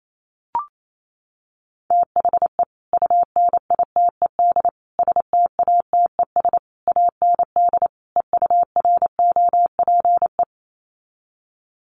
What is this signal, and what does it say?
Morse code, heard as a keyed tone, about 28 wpm, 700 Hz. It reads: THE UNITED STATES AND EUROPE
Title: Sets of 5 Words Encore 28wpm